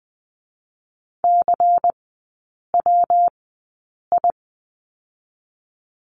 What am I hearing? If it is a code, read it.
Morse code, heard as a keyed tone, about 20 wpm, spaced as for 8 wpm, 700 Hz. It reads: CWI
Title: 121 Lesson 20e - Number 3 - Three Characters 20wpm